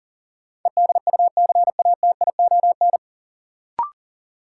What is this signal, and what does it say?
Morse code, heard as a keyed tone, about 40 wpm, 700 Hz. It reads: EDUCATION
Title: Sentences from Top 1000 Words - Review Part 6 - New Word Drill - 40wpm